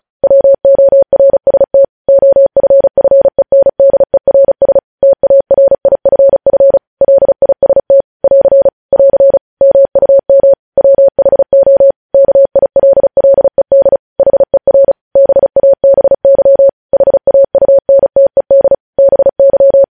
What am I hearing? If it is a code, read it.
Morse code, heard as a keyed tone, about 36 wpm, 550 Hz. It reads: WORST OFFENDERS TARIFF LIST <AR> <AR> MUM WHO KILLED HER BABY HAUNTED BY
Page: news headlines in morse code at 35 wpm